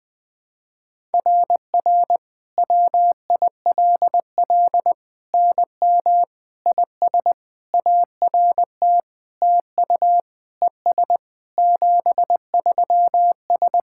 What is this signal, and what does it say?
Morse code, heard as a keyed tone, about 20 wpm, 700 Hz. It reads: RR WILL NM IS ART TU ES 73S
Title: SOTA QSO Practice 20wpm